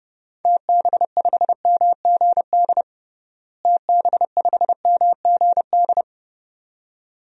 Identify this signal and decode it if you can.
Morse code, heard as a keyed tone, about 30 wpm, 700 Hz. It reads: TB5MGD TB5MGD
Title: DXCC-100 Call Signs - 6 Characters - Repeated 2x - 30wpm